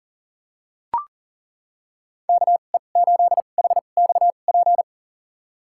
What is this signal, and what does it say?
Morse code, heard as a keyed tone, about 40 wpm, spaced as for 29 wpm, 700 Hz. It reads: KE8HXP
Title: USA Call Signs - 2x Character Spacing - 40wpm